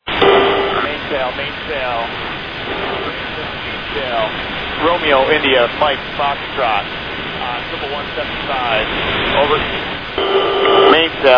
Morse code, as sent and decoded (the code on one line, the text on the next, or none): none